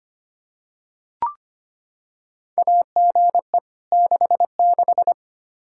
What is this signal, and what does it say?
Morse code, heard as a keyed tone, about 25 wpm, 700 Hz. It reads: AGE 66
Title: K1USN SST QSO Practice 25wpm